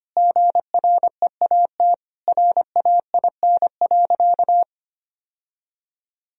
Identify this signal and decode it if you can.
Morse code, heard as a keyed tone, about 25 wpm, 700 Hz. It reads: GREAT RAIN.
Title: Sentences from Top 1000 Words - Review Part 3 - 25wpm